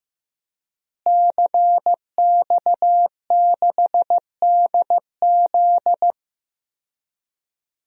Morse code, heard as a keyed tone, about 15 wpm, 700 Hz. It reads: CX6DZ
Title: DXCC-100 Call Signs - 5 Characters - Repeated 3x - 15wpm